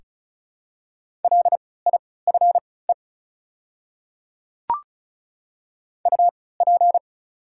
Morse code, heard as a keyed tone, about 36 wpm, spaced as for 20 wpm, 690 Hz. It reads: LIFE UP
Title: 3x Character Spacing - Top 300 Words - 35wpm